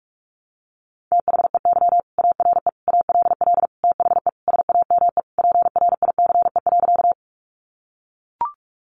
Morse code, heard as a keyed tone, about 44 wpm, 700 Hz. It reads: THEY ARE ALL THE SAME PRICE.
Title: Sentences from Top 600 Words 45wpm